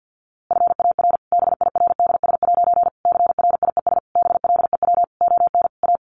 Morse code, heard as a keyed tone, about 50 wpm, 700 Hz. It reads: FAR BIRDS' CRIES BLEW ON A
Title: A Sound Of Thunder 50wpm